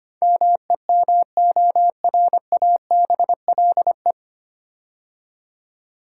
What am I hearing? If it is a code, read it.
Morse code, heard as a keyed tone, about 25 wpm, 700 Hz. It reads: MEMORABLE
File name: practice / 25wpm / 9 Letter Words - First 3 Letters Given - 25wpm